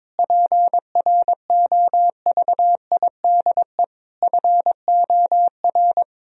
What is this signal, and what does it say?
Morse code, heard as a keyed tone, about 21 wpm, 700 Hz. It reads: PROVIDE FOR
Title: Sentences from Top 1000 Words - Review Part 3 - 22wpm